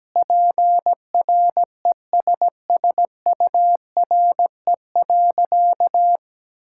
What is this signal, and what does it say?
Morse code, heard as a keyed tone, about 17 wpm, 700 Hz. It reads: PRESSURE.
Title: Sentences from Top 1000 Words - Review Part 6 - 17wpm